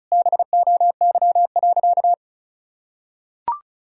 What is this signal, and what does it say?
Morse code, heard as a keyed tone, about 34 wpm, 700 Hz. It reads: BOY.
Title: Sentences from Top 1000 Words - Review Part 1 - 35wpm